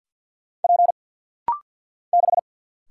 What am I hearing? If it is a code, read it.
Morse code, heard as a keyed tone, about 49 wpm, 690 Hz. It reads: P 6
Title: Single Letter-Number - Rapid-Fire 50wpm